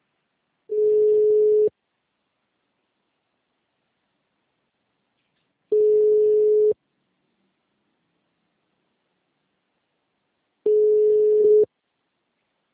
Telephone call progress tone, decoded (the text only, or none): ringback tone